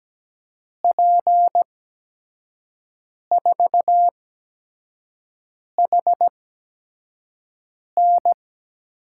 Morse code, heard as a keyed tone, about 17 wpm, spaced as for 5 wpm, 700 Hz. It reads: P4HN
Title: USA Call Signs - 6 Characters - 8x Character Spacing - 17wpm